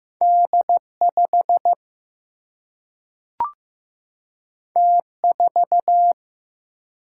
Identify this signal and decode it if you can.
Morse code, heard as a keyed tone, about 15 wpm, 700 Hz. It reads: D 5 T 4
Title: Single-Letter Single-Number 15wpm